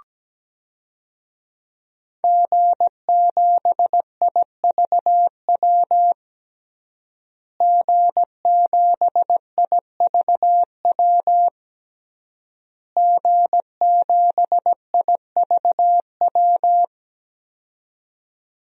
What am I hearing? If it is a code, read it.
Morse code, heard as a keyed tone, about 17 wpm, 700 Hz. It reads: G7IVW G7IVW G7IVW
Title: DXCC-100 Call Signs - 5 Characters - Repeated 3x - 17wpm